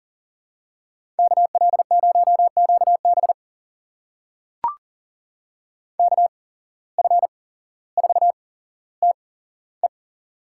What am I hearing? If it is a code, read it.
Morse code, heard as a keyed tone, about 40 wpm, spaced as for 17 wpm, 700 Hz. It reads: KL0QB KF4TE
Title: USA Call Signs - 5 Characters - 8x Character Spacing - 40wpm